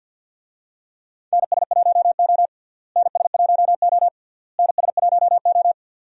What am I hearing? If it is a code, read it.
Morse code, heard as a keyed tone, about 52 wpm, 700 Hz. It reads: NS1O NS1O NS1O
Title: USA Call Signs - 4 Characters - Repeated 3x - 50wpm